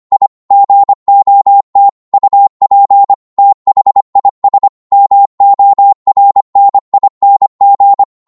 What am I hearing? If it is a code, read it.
Morse code, heard as a keyed tone, about 25 wpm, 800 Hz. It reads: I GOT UP THIS MORNING